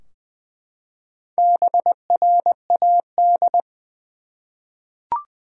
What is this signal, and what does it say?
Morse code, heard as a keyed tone, about 20 wpm, 700 Hz. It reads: BRAD
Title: USA Common Names - Max 4 Letters - 20wpm